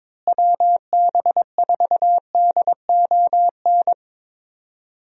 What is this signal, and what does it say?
Morse code, heard as a keyed tone, about 22 wpm, 700 Hz. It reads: WB4DON
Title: USA Call Signs - 6 Characters - Repeated 2x - 22wpm